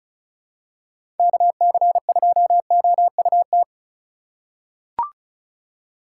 Morse code, heard as a keyed tone, about 34 wpm, 700 Hz. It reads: KC2OUT
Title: USA Call Signs - 6 Characters - 2x Character Spacing - 35wpm